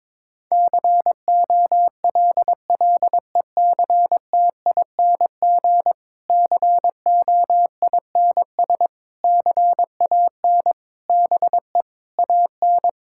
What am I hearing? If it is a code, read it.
Morse code, heard as a keyed tone, about 22 wpm, 700 Hz. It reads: COLLECTING COINS CAN BE AN
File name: Difficult Sentences with Hints - 22wpm